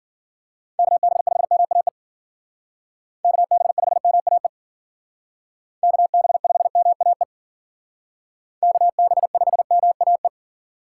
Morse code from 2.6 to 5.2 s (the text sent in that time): KB5MAE